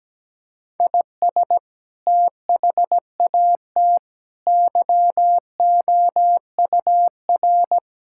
Morse code, heard as a keyed tone, about 17 wpm, 700 Hz. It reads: IS THAT YOUR
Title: Sentences from Top 1000 Words - Review Part 6 - 17wpm